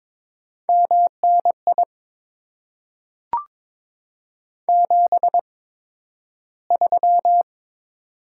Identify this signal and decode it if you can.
Morse code, heard as a keyed tone, about 22 wpm, spaced as for 12 wpm, 700 Hz. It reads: MNI 73
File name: Top 90 QSO Elements - 8x Character Spacing - 22wpm